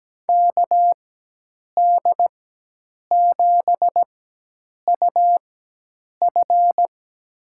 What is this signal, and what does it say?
Morse code, heard as a keyed tone, about 17 wpm, spaced as for 8 wpm, 700 Hz. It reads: KD7UF
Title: USA Call Signs - 5 Characters - 4x Character Spacing - 17wpm